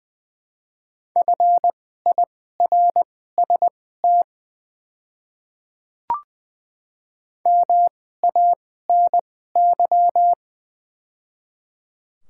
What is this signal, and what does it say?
Morse code, heard as a keyed tone, about 20 wpm, spaced as for 15 wpm, 690 Hz. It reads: FIRST MANY
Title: 2x Character Spacing - Top 100 Words - 20wpm